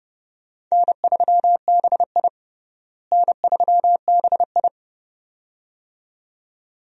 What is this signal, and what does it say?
Morse code, heard as a keyed tone, about 32 wpm, 700 Hz. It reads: N3BI N3BI
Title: USA Call Signs - 4 Characters - Repeated 2x - 30wpm